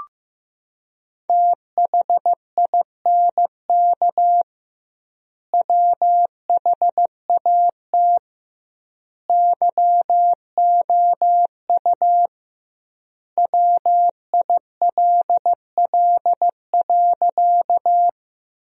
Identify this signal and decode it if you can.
Morse code, heard as a keyed tone, about 15 wpm, 700 Hz. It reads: THINK WHAT YOU WILL.
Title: 2x Word-Spacing - Sentences from Top 200 Words - 15wpm